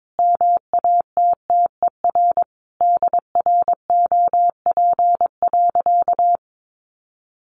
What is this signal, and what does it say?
Morse code, heard as a keyed tone, about 22 wpm, 700 Hz. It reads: MATTER DROP.
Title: Sentences from Top 1400 Words 22wpm